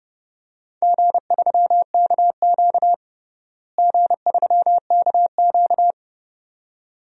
Morse code, heard as a keyed tone, about 32 wpm, 700 Hz. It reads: G3KQ G3KQ
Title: DXCC-100 Call Signs - 4 Characters - Repeated 2x - 30wpm